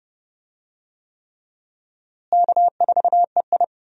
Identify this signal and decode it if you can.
Morse code, heard as a keyed tone, about 31 wpm, 700 Hz. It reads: K4EI
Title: USA Call Signs - 4 Characters - Repeated 3x - 30wpm